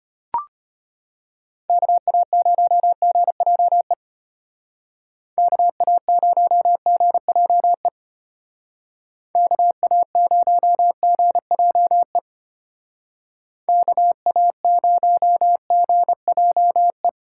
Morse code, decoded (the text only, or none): KA0GJE KA0GJE KA0GJE KA0GJE